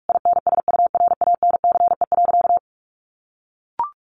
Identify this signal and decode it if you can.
Morse code, heard as a keyed tone, about 44 wpm, 700 Hz. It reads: INSURANCE.
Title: A Sound Of Thunder 45wpm